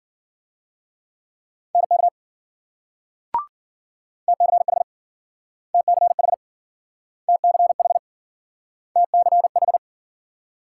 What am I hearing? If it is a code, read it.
Morse code, heard as a keyed tone, about 55 wpm, 700 Hz. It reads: NK TCH TCH TCH TCH